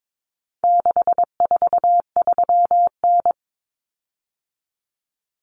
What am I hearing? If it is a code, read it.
Morse code, heard as a keyed tone, about 22 wpm, 700 Hz. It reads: 643N
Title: Common Cut Numbers - 4-Digit 22wpm